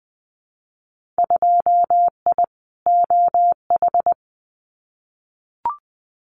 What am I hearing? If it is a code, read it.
Morse code, heard as a keyed tone, about 20 wpm, 700 Hz. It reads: 2I OH
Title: Winter Field Day Exchanges 20wpm